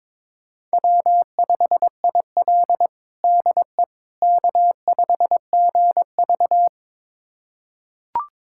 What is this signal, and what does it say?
Morse code, heard as a keyed tone, about 22 wpm, 700 Hz. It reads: W5IL DE K5GV
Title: USA Callsign DE Callsign - 4 Characters - 22wpm